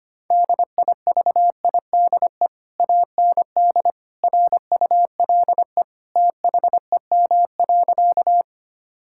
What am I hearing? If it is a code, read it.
Morse code, heard as a keyed tone, about 25 wpm, 700 Hz. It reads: DIVIDE AND RULE THEM.
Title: Sentences from Top 1600 Words 25wpm